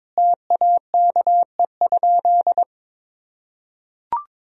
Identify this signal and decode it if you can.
Morse code, heard as a keyed tone, about 22 wpm, 700 Hz. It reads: TAKE?
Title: Sentences from Top 1000 Words - Review Part 1 - 22wpm